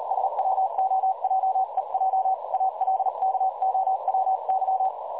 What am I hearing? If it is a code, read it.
Morse code, heard as a keyed tone, about 39 wpm, 760 Hz, with noise. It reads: SWOJEJ ALKOWY
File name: dlazaawansowanych / nagraniazpasma / websdr recording 3565.0kHz